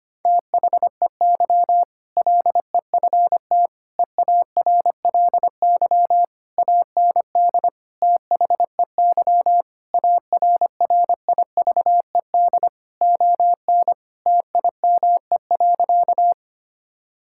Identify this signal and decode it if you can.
Morse code, heard as a keyed tone, about 25 wpm, 700 Hz. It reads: THEY LEFT EARLY AND THEY ARRIVED ON TIME.
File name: practice / 25wpm / Sentences with 2-Word Phrases - 25wpm